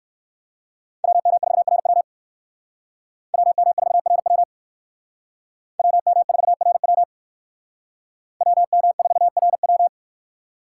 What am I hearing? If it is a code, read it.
Morse code, heard as a keyed tone, about 58 wpm, 700 Hz. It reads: WM4RW WM4RW WM4RW WM4RW